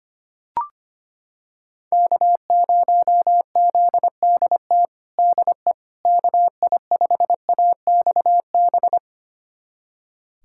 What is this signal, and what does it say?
Morse code, heard as a keyed tone, about 25 wpm, 700 Hz. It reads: K0ZDT DE KI5AXB
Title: USA Callsign DE Callsign 25wpm